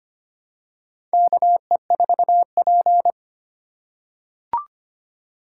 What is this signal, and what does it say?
Morse code, heard as a keyed tone, about 25 wpm, 700 Hz. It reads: KE4P
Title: USA Call Signs - 4 Characters - 2x Character Spacing - 25wpm